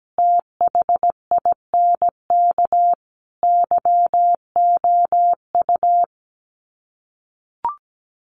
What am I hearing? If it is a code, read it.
Morse code, heard as a keyed tone, about 17 wpm, 700 Hz. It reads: THINK YOU
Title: Sets of 4 Words 17wpm